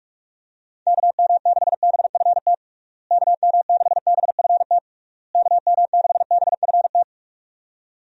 Morse code, heard as a keyed tone, about 44 wpm, 700 Hz. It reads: KM6BFT KM6BFT KM6BFT
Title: USA Call Signs - 6 Characters - Repeated 3x - 45wpm